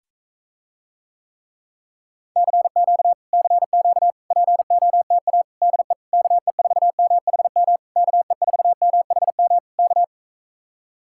Morse code, heard as a keyed tone, about 44 wpm, 700 Hz. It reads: CQ CQ POTA DE KE4MHM KE4MHM K